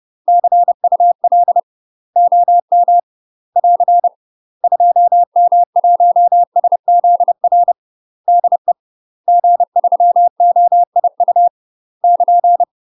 Morse code, heard as a keyed tone, about 30 wpm, 700 Hz. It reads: CUL OM <AR> 2M1SZR DE G3OIU <KN>
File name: UK MASTER QSOs TRAINING / 30WPM / UK MASTER QSOs 0045